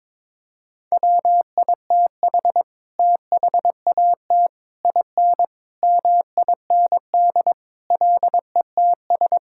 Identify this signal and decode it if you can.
Morse code, heard as a keyed tone, about 22 wpm, 700 Hz. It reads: WITH THAT IN MIND LETS
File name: Sets of 9 Words - 22wpm